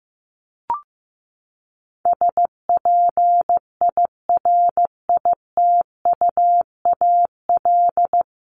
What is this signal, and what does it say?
Morse code, heard as a keyed tone, about 15 wpm, 700 Hz. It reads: SPIRITUAL
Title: Top 1501-1750 Words 15wpm